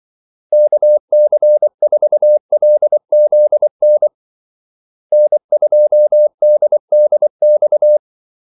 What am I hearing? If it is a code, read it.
Morse code, heard as a keyed tone, about 24 wpm, 600 Hz. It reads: KC4LZN N2DDX
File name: Callsigns-24wpm